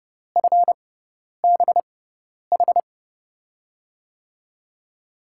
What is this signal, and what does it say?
Morse code, heard as a keyed tone, about 30 wpm, spaced as for 10 wpm, 700 Hz. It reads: FBH
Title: USA Call Signs - 6 Characters - 6x Character Spacing - 30wpm